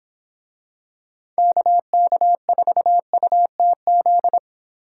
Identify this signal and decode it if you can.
Morse code, heard as a keyed tone, about 25 wpm, 700 Hz. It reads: KK4UTZ